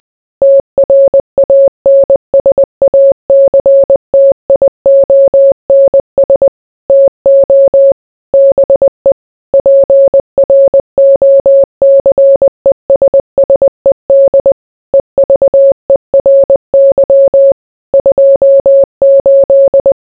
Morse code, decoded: TRANSACTIONS TO BE PROCESSED EVERY 28

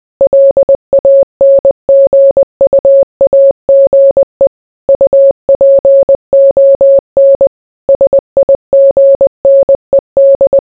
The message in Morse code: LANGUAGE UPON SIGNED

